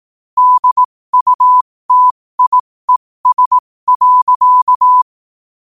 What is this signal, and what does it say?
Morse code, heard as a keyed tone, about 18 wpm, spaced as for 15 wpm, 1000 Hz. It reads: DUTIES.
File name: qotdfiles / 15wpm